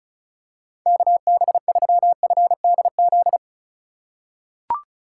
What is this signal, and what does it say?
Morse code, heard as a keyed tone, about 36 wpm, 700 Hz. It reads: KB3FDZ
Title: USA Call Signs - 6 Characters - 2x Character Spacing - 35wpm